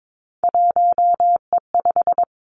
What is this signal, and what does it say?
Morse code, heard as a keyed tone, about 22 wpm, 700 Hz. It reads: 1E5
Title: Three Challenging Letter-Number Combinations 22wpm